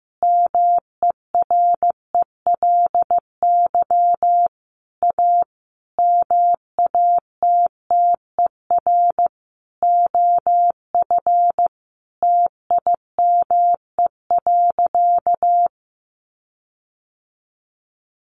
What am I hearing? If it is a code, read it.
Morse code, heard as a keyed tone, about 15 wpm, 700 Hz. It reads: MERELY A MATTER OF TIME.
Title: Sentences from Top 1400 Words 15wpm